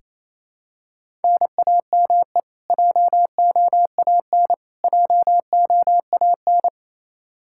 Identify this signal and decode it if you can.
Morse code, heard as a keyed tone, about 27 wpm, 700 Hz. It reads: NAME JOAN JOAN
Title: Practice QSO - Exchange 1 - Beginning - 28wpm